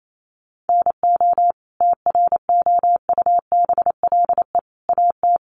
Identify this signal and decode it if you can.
Morse code, heard as a keyed tone, about 28 wpm, 700 Hz. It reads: NO TROUBLE AT
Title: Sentences from Top 1200 Words 28wpm